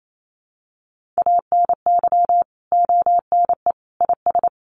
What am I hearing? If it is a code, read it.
Morse code, heard as a keyed tone, about 28 wpm, 700 Hz. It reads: ANY ONE IS
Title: Sentences from Top 1200 Words 28wpm